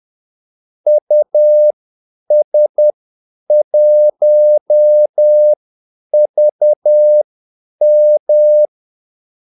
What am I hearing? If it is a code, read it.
Morse code, heard as a keyed tone, about 10 wpm, spaced as for 8 wpm, 600 Hz. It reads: US1VM
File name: Callsigns-10-8wpm